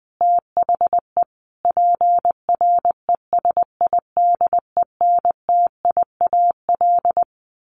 Morse code, heard as a keyed tone, about 20 wpm, 700 Hz. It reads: THE PRESIDENTIAL
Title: A Sound Of Thunder 20wpm